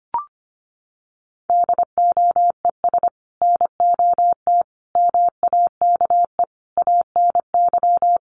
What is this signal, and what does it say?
Morse code, heard as a keyed tone, about 25 wpm, 700 Hz. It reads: DOES NOT MAKE ANY
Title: Sets of 5 Words Encore2 25wpm